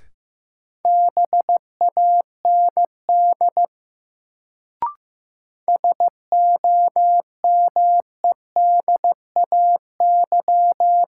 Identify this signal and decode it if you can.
Morse code, heard as a keyed tone, about 15 wpm, 700 Hz. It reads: BAND SOMEDAY
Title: Sentences from Top 1600 Words - New Word Drill Alt - 15wpm